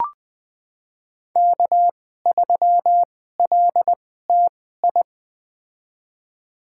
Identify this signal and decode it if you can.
Morse code, heard as a keyed tone, about 21 wpm, spaced as for 15 wpm, 700 Hz. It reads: K3LTI
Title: USA Call Signs - 2x Character Spacing - 20wpm